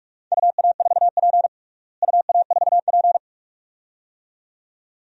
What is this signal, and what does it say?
Morse code, heard as a keyed tone, about 46 wpm, 700 Hz. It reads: UA4P UA4P
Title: DXCC-100 Call Signs - 4 Characters - Repeated 2x - 45wpm